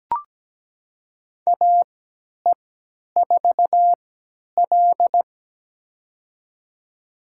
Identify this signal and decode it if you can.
Morse code, heard as a keyed tone, about 17 wpm, spaced as for 10 wpm, 700 Hz. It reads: AE4L
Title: USA Call Signs - 3x Character Spacing - 17wpm